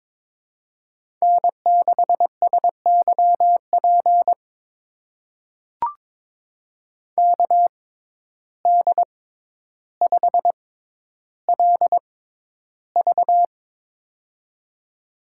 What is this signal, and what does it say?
Morse code, heard as a keyed tone, about 22 wpm, spaced as for 11 wpm, 700 Hz. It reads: N6SYP KD5LV
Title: USA Call Signs - 5 Characters - 6x Character Spacing - 22wpm